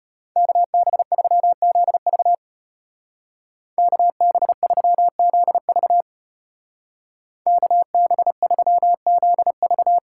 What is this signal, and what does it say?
Morse code, heard as a keyed tone, about 33 wpm, 700 Hz. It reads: KB3ZV KB3ZV KB3ZV